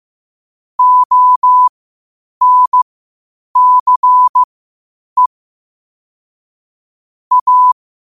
Morse code, heard as a keyed tone, about 15 wpm, spaced as for 8 wpm, 1000 Hz. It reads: ONCE A
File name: qotdfiles / 8wpm